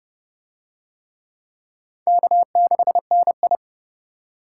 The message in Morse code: K6NI

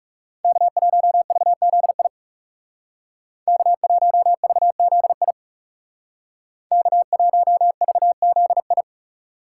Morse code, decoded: K1VZI K1VZI K1VZI

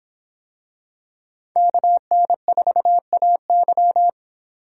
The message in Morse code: KN4AY

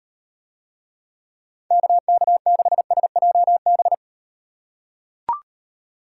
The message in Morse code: KK6SJB

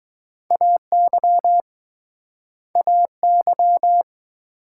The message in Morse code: AY AY

